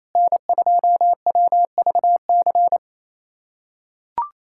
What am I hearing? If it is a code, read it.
Morse code, heard as a keyed tone, about 28 wpm, 700 Hz. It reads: N2WVC
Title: USA Callsign DE Callsign 28wpm